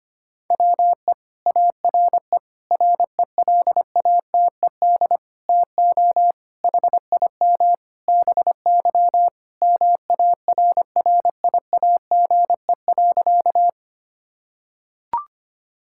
Morse code, heard as a keyed tone, about 25 wpm, 700 Hz. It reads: WE ARE RELATED TO HIM BY MARRIAGE.
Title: Sentences from Top 1000 Words - Review Part 5 - 25wpm